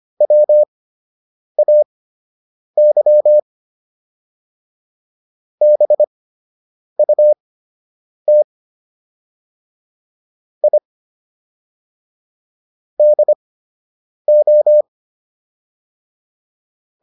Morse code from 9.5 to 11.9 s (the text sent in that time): I